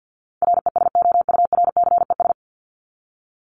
Morse code, heard as a keyed tone, about 48 wpm, 700 Hz. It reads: RESOURCES
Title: ES:S Suffix - Top 100 Words with Base Words 50wpm